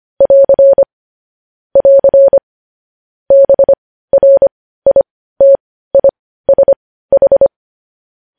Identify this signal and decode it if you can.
Morse code, heard as a keyed tone, about 25 wpm, spaced as for 15 wpm, 550 Hz. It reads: <AR> <AR> BRITISH
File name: podcast / News headlines-1x-25x15